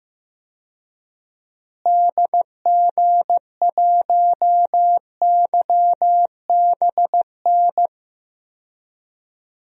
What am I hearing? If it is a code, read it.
Morse code, heard as a keyed tone, about 16 wpm, 700 Hz. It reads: DG1YBN
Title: DXCC-100 Call Signs - 6 Characters - Repeated 2x - 15wpm